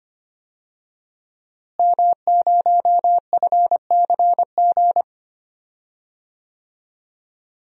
Morse code, heard as a keyed tone, about 25 wpm, 700 Hz. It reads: M0FCG
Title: DXCC-100 Call Signs - 5 Characters - Repeated 2x - 25wpm